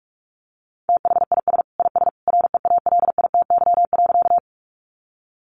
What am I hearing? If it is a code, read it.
Morse code, heard as a keyed tone, about 44 wpm, 700 Hz. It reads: THIS IS REALITY.
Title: Sentences from Top 1200 Words 45wpm